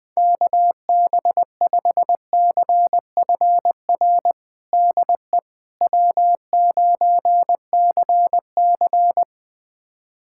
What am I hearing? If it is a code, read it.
Morse code, heard as a keyed tone, about 20 wpm, 700 Hz. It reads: KB5CFR DE W9CC
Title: USA Callsign DE Callsign 20wpm